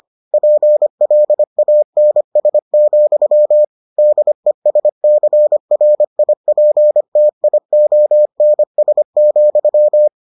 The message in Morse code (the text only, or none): PLANS, DESCRIPTIONS,